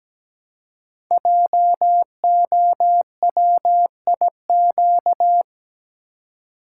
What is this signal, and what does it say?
Morse code, heard as a keyed tone, about 17 wpm, 700 Hz. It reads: JOWIQ
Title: Code Groups - 5 Letters - 17wpm